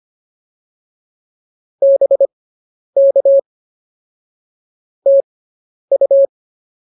Morse code, heard as a keyed tone, about 25 wpm, spaced as for 10 wpm, 550 Hz. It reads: BK TU